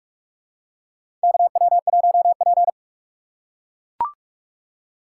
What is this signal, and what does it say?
Morse code, heard as a keyed tone, about 43 wpm, 700 Hz. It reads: KW1P
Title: USA Call Signs - 6x Character Spacing - 45wpm